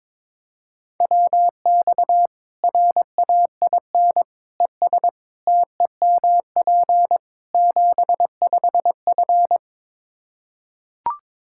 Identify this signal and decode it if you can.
Morse code, heard as a keyed tone, about 22 wpm, 700 Hz. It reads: WX RAIN ES TEMP 75F
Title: Practice QSO - Exchange 2 - Info - 22wpm